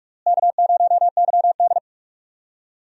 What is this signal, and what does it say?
Morse code, heard as a keyed tone, about 46 wpm, 700 Hz. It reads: K0YD
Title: USA Call Signs - 4 Characters - Repeated 3x - 45wpm